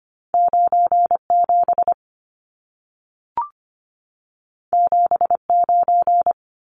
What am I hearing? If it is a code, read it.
Morse code, heard as a keyed tone, about 25 wpm, 700 Hz. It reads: 97 79
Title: Numbers - 2-Digit 25wpm